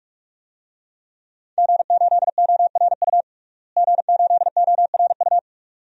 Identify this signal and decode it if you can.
Morse code, heard as a keyed tone, about 41 wpm, 700 Hz. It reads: G8ORU G8ORU
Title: DXCC-100 Call Signs - 5 Characters - Repeated 2x - 45wpm